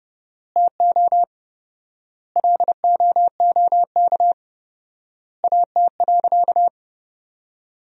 Morse code, heard as a keyed tone, about 30 wpm, 700 Hz. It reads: TO LOOK AT.
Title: 4x Word-Spacing - Sentences from Top 200 Words - 30wpm